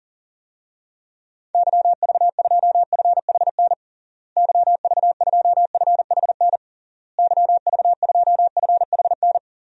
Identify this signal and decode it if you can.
Morse code, heard as a keyed tone, about 40 wpm, 700 Hz. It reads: YV2FHN YV2FHN YV2FHN
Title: DXCC-100 Call Signs - 6 Characters - Repeated 3x - 40wpm